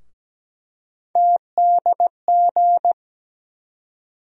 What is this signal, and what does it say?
Morse code, heard as a keyed tone, about 17 wpm, 700 Hz. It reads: TDG
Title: USA Call Sign Postfixes - 3 Letters 17wpm